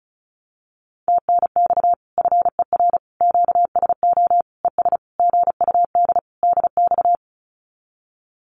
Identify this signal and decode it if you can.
Morse code, heard as a keyed tone, about 34 wpm, 700 Hz. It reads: TNX FER QSO ES GUD DX
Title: Common QSO Phrases Encore 35wpm